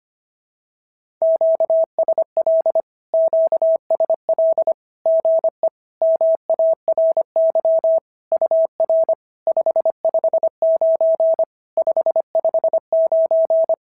Morse code, heard as a keyed tone, about 24 wpm, 650 Hz. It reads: QSL QSL GE MARY UR 559 559
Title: SOTA QSO Practice 25wpm